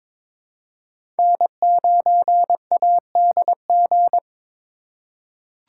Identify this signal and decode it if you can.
Morse code, heard as a keyed tone, about 22 wpm, 700 Hz. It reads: N9ADG
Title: K1USN SST QSO Practice - 22wpm